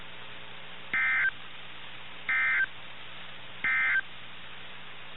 Emergency Alert System (SAME) tones, heard as harsh buzzing bursts. It is heard from 0.8 to 4.1 s.
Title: NOAA Weather Radio - KIH61